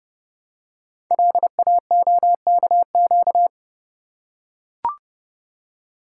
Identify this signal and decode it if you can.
Morse code, heard as a keyed tone, about 30 wpm, 700 Hz. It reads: LAOKQ
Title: Code Groups - 5 Letters - 30wpm